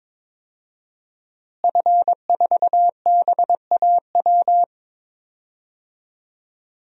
Morse code, heard as a keyed tone, about 22 wpm, 700 Hz. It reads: F4BAW
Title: DXCC-100 Call Signs - 5 Characters - Repeated 3x - 22wpm